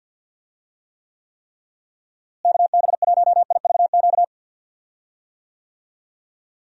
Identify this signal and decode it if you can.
Morse code, heard as a keyed tone, about 48 wpm, 700 Hz. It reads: KB1IVQ
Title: USA Call Signs - 6 Characters - Repeated 3x - 50wpm